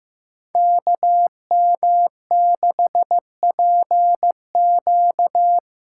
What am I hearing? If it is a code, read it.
Morse code, heard as a keyed tone, about 15 wpm, 700 Hz. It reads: KM6PQ
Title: USA Call Signs - 6 Characters - 6x Character Spacing - 15wpm